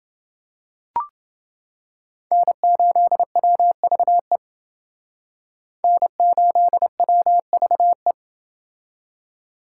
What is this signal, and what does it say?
Morse code, heard as a keyed tone, about 28 wpm, 700 Hz. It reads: N8WVE N8WVE